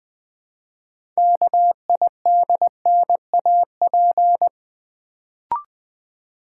Morse code, heard as a keyed tone, about 20 wpm, 700 Hz. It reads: KIDNAP
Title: Build A Word - 3 Letters At A Time - 6-Letter Words - 20wpm